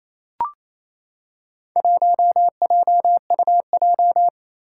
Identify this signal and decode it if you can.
Morse code, heard as a keyed tone, about 27 wpm, 700 Hz. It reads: 1JUJ
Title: Four Challenging Letter-Number Combinations 28wpm